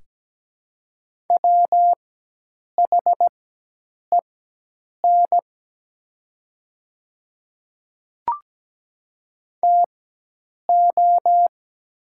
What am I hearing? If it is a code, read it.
Morse code, heard as a keyed tone, about 17 wpm, spaced as for 8 wpm, 700 Hz. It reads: WHEN TO
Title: 4x Character Spacing - Top 100 Words - 17wpm